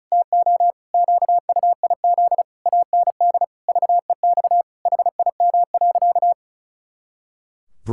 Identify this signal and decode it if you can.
Morse code, heard as a keyed tone, about 36 wpm, 700 Hz. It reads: TO QUIZ AND VEX HIM.
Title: Pangram Challenge v2 - 35wpm